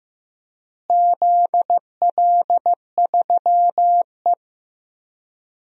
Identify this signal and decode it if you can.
Morse code, heard as a keyed tone, about 16 wpm, 700 Hz. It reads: ZL3E
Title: DXCC-100 Call Signs - 4 Characters - Repeated 3x - 15wpm